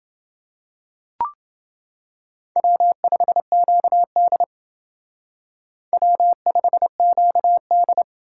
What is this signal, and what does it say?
Morse code, heard as a keyed tone, about 28 wpm, 700 Hz. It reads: W5QD W5QD